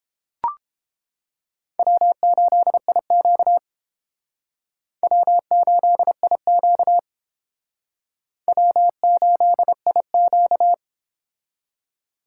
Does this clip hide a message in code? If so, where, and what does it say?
Yes: Morse code from 1.8 to 10.7 s, heard as a keyed tone, about 30 wpm, 700 Hz. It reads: W8IQ W8IQ W8IQ